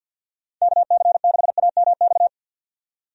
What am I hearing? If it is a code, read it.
Morse code, heard as a keyed tone, about 51 wpm, 700 Hz. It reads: KC6AMX
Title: USA Call Signs - 6 Characters - 50wpm